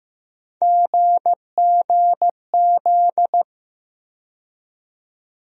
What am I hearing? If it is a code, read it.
Morse code, heard as a keyed tone, about 15 wpm, 700 Hz. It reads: GGZ
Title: USA Call Sign Postfixes - 3 Letters 15wpm